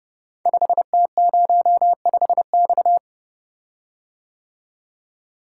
Morse code, heard as a keyed tone, about 30 wpm, 700 Hz. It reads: 5T05X
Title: Code Groups - 5 Letter:Numbers - 30wpm